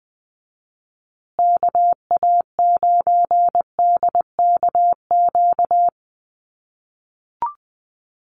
Morse code, heard as a keyed tone, about 20 wpm, 700 Hz. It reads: KA9DKQ
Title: USA Call Signs Repeated 3x 20wpm